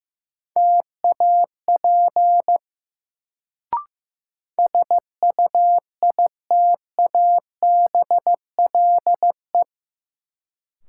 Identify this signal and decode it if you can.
Morse code, heard as a keyed tone, about 15 wpm, 700 Hz. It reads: TAP SUITABLE
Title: Sentences from Top 1400 Words - New Word Drill Alt - 15wpm